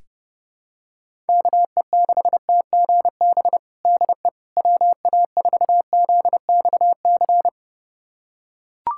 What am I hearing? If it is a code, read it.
Morse code, heard as a keyed tone, about 30 wpm, 700 Hz. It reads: KE6TGB DE WA4ZXC
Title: USA Callsign DE Callsign 30wpm